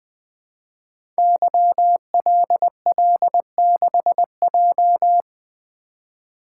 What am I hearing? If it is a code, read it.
Morse code, heard as a keyed tone, about 20 wpm, 700 Hz. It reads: YLL6J
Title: Code Groups - 5 Letter:Numbers - 20wpm